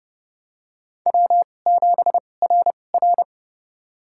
Morse code, heard as a keyed tone, about 28 wpm, spaced as for 21 wpm, 690 Hz. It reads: W7RR
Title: USA Call Signs - 4 Characters - 2x Character Spacing - 30wpm